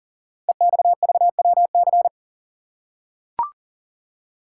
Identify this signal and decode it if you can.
Morse code, heard as a keyed tone, about 40 wpm, 700 Hz. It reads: EXVWC
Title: Code Groups - 5 Letters - 40wpm